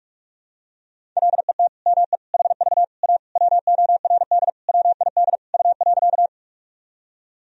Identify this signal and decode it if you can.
Morse code, heard as a keyed tone, about 46 wpm, 700 Hz. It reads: LET ME HV A WORD WID U.
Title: Sentences from Top 300 Words and Abbreviations 45wpm